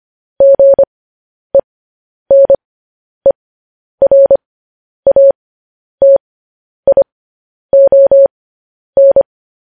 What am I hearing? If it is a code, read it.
Morse code, heard as a keyed tone, about 25 wpm, spaced as for 10 wpm, 550 Hz. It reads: GENERATION
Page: News Headlines in Morse Code at 10 WPM